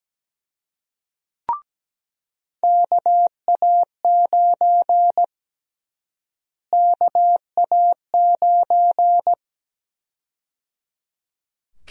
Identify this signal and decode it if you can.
Morse code, heard as a keyed tone, about 17 wpm, 700 Hz. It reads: KA9 KA9
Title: USA Call Sign Prefixes - Repeated 2x - 17wpm